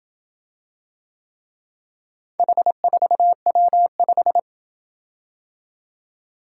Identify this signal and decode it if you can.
Morse code, heard as a keyed tone, about 27 wpm, 700 Hz. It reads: H4W5